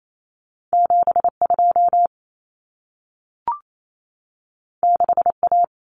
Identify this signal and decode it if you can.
Morse code, heard as a keyed tone, about 28 wpm, 700 Hz. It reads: 72 6A
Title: Common Cut Numbers - 2-Digit 28wpm